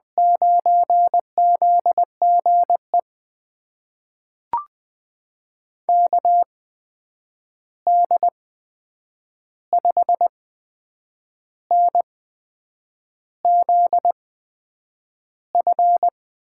Morse code, heard as a keyed tone, about 20 wpm, spaced as for 8 wpm, 700 Hz. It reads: 9ZGE KD5NZF
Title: USA Call Signs - 8x Character Spacing - 20wpm